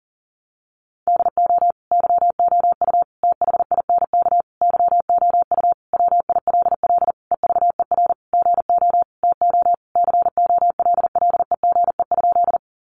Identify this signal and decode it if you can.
Morse code, heard as a keyed tone, about 40 wpm, 700 Hz. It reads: DO YOU THINK YOU WILL EVER GO TO COLLEGE?
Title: Sentences from Top 1300 Words 40wpm